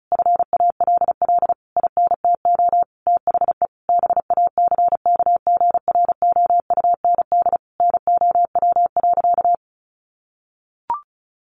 Morse code, heard as a keyed tone, about 34 wpm, 700 Hz. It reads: FALL INTO THE BACKGROUND NOW.
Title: Sentences from Top 1300 Words 35wpm